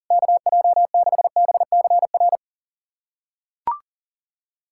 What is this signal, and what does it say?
Morse code, heard as a keyed tone, about 40 wpm, 700 Hz. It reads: KJ6BCR
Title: USA Callsign DE Callsign - 6 Characters - 40wpm